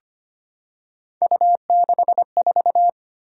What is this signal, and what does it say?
Morse code, heard as a keyed tone, about 25 wpm, 700 Hz. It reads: U64
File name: Three Letter-Number Combinations - 25wpm